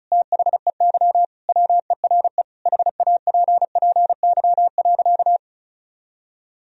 Morse code, heard as a keyed tone, about 36 wpm, 700 Hz. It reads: THEY WERE HAPPY.
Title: Sentences from Top 1000 Words - Review Part 1 - 35wpm